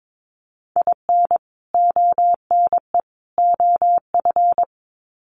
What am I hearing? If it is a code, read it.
Morse code, heard as a keyed tone, about 22 wpm, 700 Hz. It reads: IN ONE OF
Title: Sets of 3 Words 22wpm